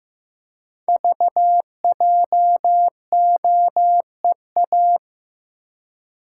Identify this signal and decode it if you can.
Morse code, heard as a keyed tone, about 15 wpm, 700 Hz. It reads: VJOEA
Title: Code Groups - 5 Letters - 15wpm